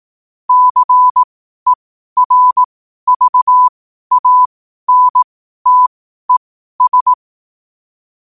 Morse code, heard as a keyed tone, about 18 wpm, spaced as for 12 wpm, 1000 Hz. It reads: CERVANTES
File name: qotdfiles / 12wpm